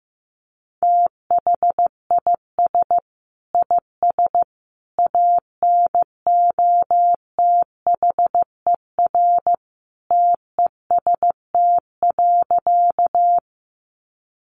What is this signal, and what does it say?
Morse code, heard as a keyed tone, about 15 wpm, 700 Hz. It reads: THIS IS ANOTHER TEST.
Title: Sentences from Top 600 Words 15wpm